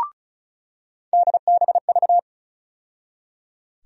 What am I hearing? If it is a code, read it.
Morse code, heard as a keyed tone, about 36 wpm, 700 Hz. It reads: DBV
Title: USA Call Sign Postfixes - 3 Letters 35wpm